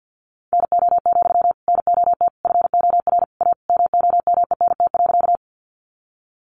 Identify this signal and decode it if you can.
Morse code, heard as a keyed tone, about 50 wpm, 700 Hz. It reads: NO, NOT FOR A MOMENT.
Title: Sentences from Top 1000 Words 50wpm